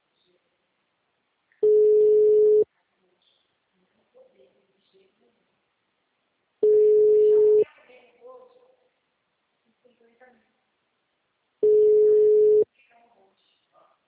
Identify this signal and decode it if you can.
Ringback tone (Europe, 425 Hz), with speech.